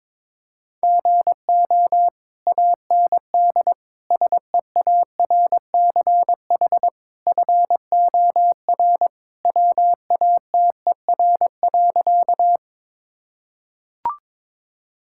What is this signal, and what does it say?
Morse code, heard as a keyed tone, about 22 wpm, 700 Hz. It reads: GO AND SEARCH FOR WATER.
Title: Sentences from Top 1000 Words - Review Part 5 - 22wpm